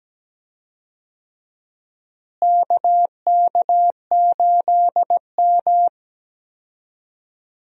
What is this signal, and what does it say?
Morse code, heard as a keyed tone, about 17 wpm, 700 Hz. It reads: KK8M
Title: USA Call Signs - 4 Characters - Repeated 3x - 17wpm